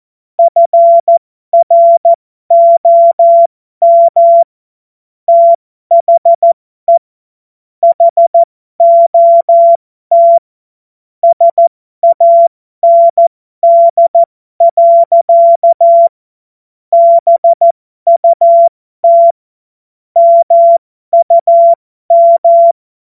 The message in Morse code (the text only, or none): FROM THE HOT SAND. BUT MUM